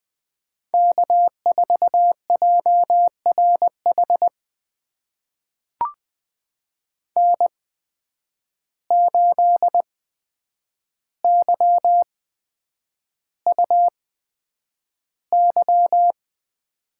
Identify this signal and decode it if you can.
Morse code, heard as a keyed tone, about 20 wpm, spaced as for 9 wpm, 700 Hz. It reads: K4JRH N8YUY
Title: USA Call Signs - 5 Characters - 8x Character Spacing - 20wpm